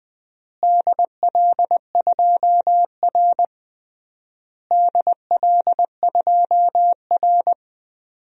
Morse code, heard as a keyed tone, about 20 wpm, 700 Hz. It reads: DL2R DL2R
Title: DXCC-100 Call Signs - 4 Characters - Repeated 3x - 20wpm